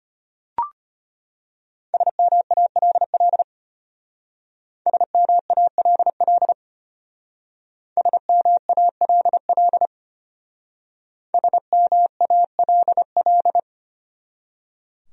Morse code, heard as a keyed tone, about 32 wpm, 700 Hz. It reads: SMALL SMALL SMALL SMALL